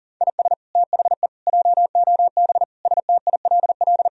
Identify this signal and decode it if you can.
Morse code, heard as a keyed tone, about 40 wpm, 700 Hz. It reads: IS THE JOB STILL
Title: Sentences from Top 1000 Words - Review Part 3 - 40wpm